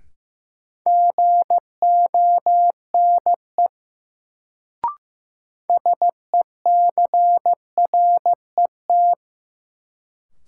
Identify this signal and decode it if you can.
Morse code, heard as a keyed tone, about 15 wpm, 700 Hz. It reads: GONE SECRET
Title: Sentences from Top 1200 Words - New Word Drill Alt - 15wpm